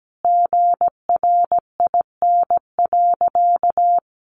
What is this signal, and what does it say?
Morse code, heard as a keyed tone, about 17 wpm, 700 Hz. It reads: GRIN.
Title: A Sound Of Thunder 17wpm